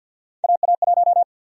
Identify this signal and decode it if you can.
Morse code, heard as a keyed tone, about 52 wpm, 700 Hz. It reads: AA1
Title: USA Call Sign Prefixes - Repeated 2x - 50wpm